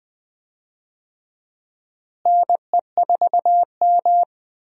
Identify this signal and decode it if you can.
Morse code, heard as a keyed tone, about 20 wpm, 700 Hz. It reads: NE4M
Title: USA Call Signs - 4 Characters - Repeated 3x - 20wpm